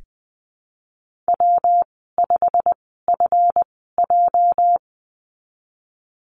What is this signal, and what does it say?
Morse code, heard as a keyed tone, about 20 wpm, spaced as for 15 wpm, 700 Hz. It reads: W5FJ
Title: 219 Lesson 35g - Number 0 - Call Signs 20wpm